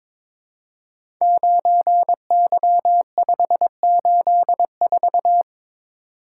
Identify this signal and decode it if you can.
Morse code, heard as a keyed tone, about 22 wpm, 700 Hz. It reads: 9Y584
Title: Code Groups - 5 Letter:Numbers - 22wpm